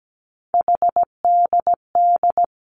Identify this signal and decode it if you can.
Morse code, heard as a keyed tone, about 17 wpm, 700 Hz. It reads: HDD
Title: Three Challenging Letter-Number Combinations 17wpm